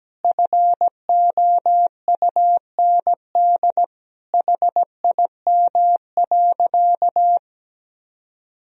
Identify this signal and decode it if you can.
Morse code, heard as a keyed tone, about 17 wpm, 700 Hz. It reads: FOUND HIM.
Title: Sentences from Top 1000 Words - Review Part 3 - 17wpm